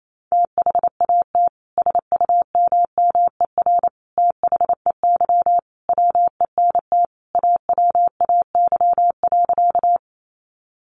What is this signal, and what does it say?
Morse code, heard as a keyed tone, about 28 wpm, 700 Hz. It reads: THAT SUMMER THEY WENT AWAY.
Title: Sentences from Top 1100 Words 28wpm